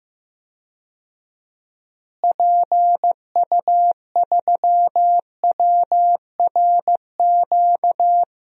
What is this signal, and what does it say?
Morse code, heard as a keyed tone, about 15 wpm, 700 Hz. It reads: PU3WRQ
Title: DXCC-100 Call Signs - 6 Characters - Repeated 2x - 15wpm